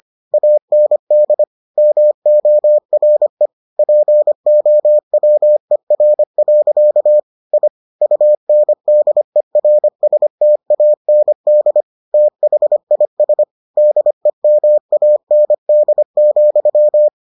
Morse code, heard as a keyed tone, about 25 wpm, 600 Hz. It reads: AND MORE POWER. I UNDERSTAND THIS DEMAND,